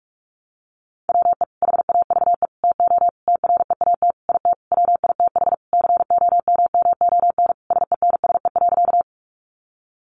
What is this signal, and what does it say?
Morse code, heard as a keyed tone, about 44 wpm, 700 Hz. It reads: WE HAVE TO TREAT IT WITH COMMON SENSE.
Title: Sentences from Top 1300 Words 45wpm